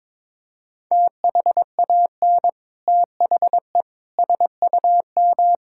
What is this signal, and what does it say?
Morse code, heard as a keyed tone, about 22 wpm, 700 Hz. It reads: THAN THE SUM
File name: practice / 22wpm / Sets of 6 Words - 22wpm